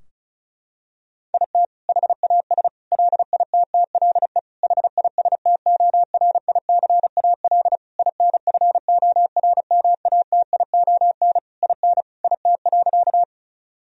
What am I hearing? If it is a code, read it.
Morse code, heard as a keyed tone, about 34 wpm, 700 Hz. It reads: IT HAS LITTLE HISTORICAL INFORMATION IN IT.
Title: Sentences from Top 1000 Words - Review Part 4 - 35wpm